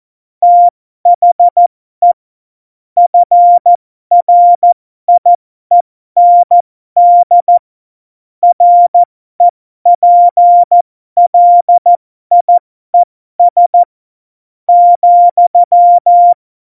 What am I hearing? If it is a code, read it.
Morse code, heard as a keyed tone, about 14 wpm, spaced as for 12 wpm, 700 Hz. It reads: THE FRIEND REPLIES,
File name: Morsekurs / Witze